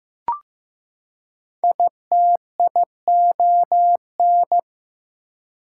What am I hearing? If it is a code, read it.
Morse code, heard as a keyed tone, about 15 wpm, 700 Hz. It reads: ITION
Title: Prefixes and Suffixes 15wpm